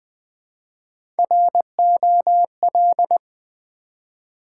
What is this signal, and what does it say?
Morse code, heard as a keyed tone, about 20 wpm, 700 Hz. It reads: ROL
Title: Build A Word - 1 Letter At A Time - 4-Letter Words - 20wpm